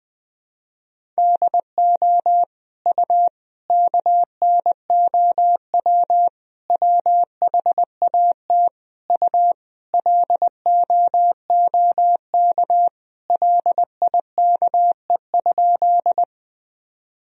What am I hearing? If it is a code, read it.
Morse code, heard as a keyed tone, about 20 wpm, 700 Hz. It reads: DO U KNOW WHAT U LOOK LIKE?
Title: Sentences from Top 100 Words and Abbreviations 20wpm